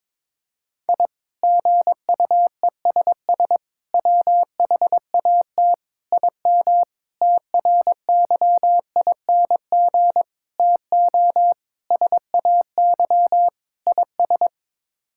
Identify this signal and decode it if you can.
Morse code, heard as a keyed tone, about 22 wpm, 700 Hz. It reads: I GUESS WHAT IM TRYING TO SAY IS
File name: practice / 22wpm / Sets of 8 Words - 22wpm